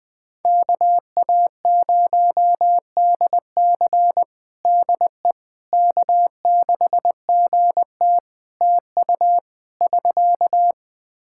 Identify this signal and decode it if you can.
Morse code, heard as a keyed tone, about 21 wpm, 700 Hz. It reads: KA0DC DE K6GT TU <SK>
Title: Practice QSO - Exchange 4 - Ending - 20wpm